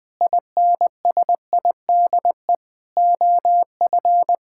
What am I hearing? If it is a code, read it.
Morse code, heard as a keyed tone, about 20 wpm, 700 Hz. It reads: INSIDE OF
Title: Sentences from Top 1000 Words - Review Part 6 - 20wpm